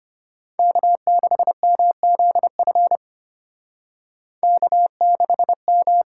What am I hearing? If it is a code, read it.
Morse code, heard as a keyed tone, about 32 wpm, 700 Hz. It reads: K6MZF K6M